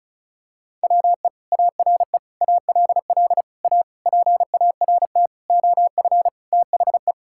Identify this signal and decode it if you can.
Morse code, heard as a keyed tone, about 36 wpm, 700 Hz. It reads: WE ARE ALL A PART OF THE
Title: Sentences from Top 1000 Words - Review Part 6 - 35wpm